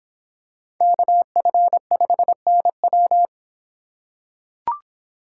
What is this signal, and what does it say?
Morse code, heard as a keyed tone, about 26 wpm, 700 Hz. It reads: KF5NW